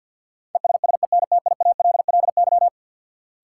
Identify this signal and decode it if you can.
Morse code, heard as a keyed tone, about 53 wpm, 700 Hz. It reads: ESSENTIALLY
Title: Sentences from Top 1300 Words - New Word Drill - 50wpm